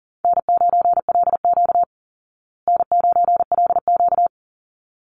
Morse code, heard as a keyed tone, about 40 wpm, 700 Hz. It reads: N9LQ N9LQ
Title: Call Signs Repeated 2x - 40wpm